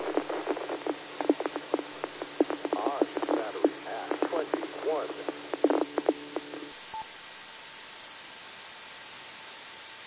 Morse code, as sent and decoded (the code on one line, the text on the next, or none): none